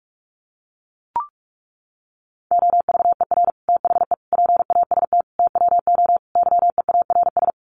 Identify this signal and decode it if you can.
Morse code, heard as a keyed tone, about 44 wpm, 700 Hz. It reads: OVER THE PAST TWO YEARS
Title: Sets of 5 Words Encore 45wpm